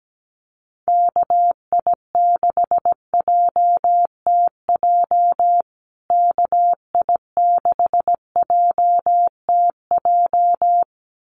Morse code, heard as a keyed tone, about 17 wpm, 700 Hz. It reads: KI6JTJ KI6JTJ
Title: USA Call Signs Repeated 2x 17wpm